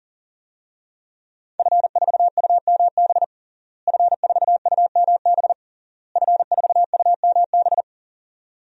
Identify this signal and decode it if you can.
Morse code, heard as a keyed tone, about 40 wpm, 700 Hz. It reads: F4UMB F4UMB F4UMB
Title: DXCC-100 Call Signs - 5 Characters - Repeated 3x - 40wpm